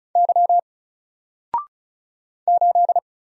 Morse code, heard as a keyed tone, about 36 wpm, 700 Hz. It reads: Y 8
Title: Single Character (Letter:Number:Punctuation:Prosign) Alt 35wpm